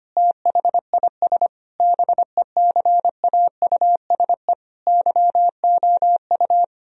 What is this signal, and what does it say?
Morse code, heard as a keyed tone, about 25 wpm, 700 Hz. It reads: THIS BECAUSE YOU
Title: Sentences from Top 1000 Words - Review Part 3 - 25wpm